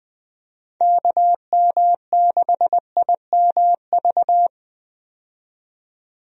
Morse code, heard as a keyed tone, about 20 wpm, 700 Hz. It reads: KM6IMV
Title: USA Call Signs - 6 Characters - 20wpm